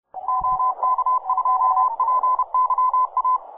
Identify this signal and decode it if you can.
Morse code, heard as a keyed tone, about 31 wpm, 1020 Hz. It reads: OK2PYA